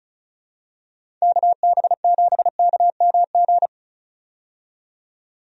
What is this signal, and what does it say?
Morse code, heard as a keyed tone, about 34 wpm, 700 Hz. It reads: KB7KMG
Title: USA Call Signs - 6 Characters - 35wpm